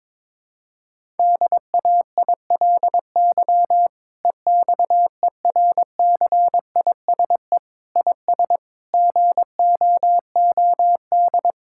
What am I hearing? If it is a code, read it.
Morse code, heard as a keyed tone, about 22 wpm, 700 Hz. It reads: DAILY EXERCISE IS GOOD
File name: Difficult Sentences with Hints - 22wpm